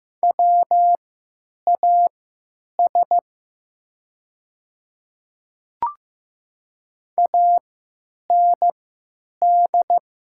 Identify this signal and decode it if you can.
Morse code, heard as a keyed tone, about 15 wpm, spaced as for 8 wpm, 700 Hz. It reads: WAS AND
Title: 3x Character Spacing - Top 100 Words - 15wpm